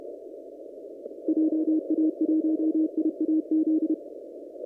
Morse code, heard as a keyed tone, about 31 wpm, 320 Hz. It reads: JA1IAZ
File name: CW / kt5j wpx cw-op-k5pi